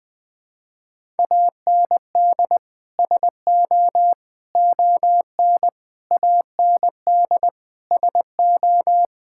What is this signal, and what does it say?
Morse code, heard as a keyed tone, about 21 wpm, 700 Hz. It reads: AND SO ON AND SO
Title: Sets of 6 Words - 20wpm